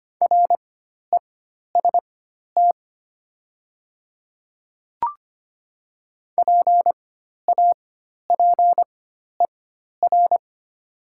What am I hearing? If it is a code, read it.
Morse code, heard as a keyed tone, about 25 wpm, spaced as for 12 wpm, 700 Hz. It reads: REST PAPER
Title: 4x Character Spacing - Top 500 Words - 25wpm